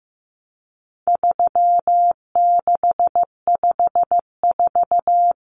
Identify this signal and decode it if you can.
Morse code, heard as a keyed tone, about 15 wpm, 700 Hz. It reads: 3654
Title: Common Cut Numbers - 4-Digit 15wpm